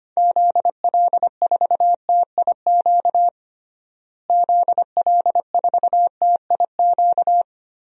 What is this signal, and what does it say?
Morse code, heard as a keyed tone, about 25 wpm, 690 Hz. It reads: ZL4TIQ ZL4TIQ
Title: DXCC-100 Call Signs - 6 Characters - Repeated 3x - 25wpm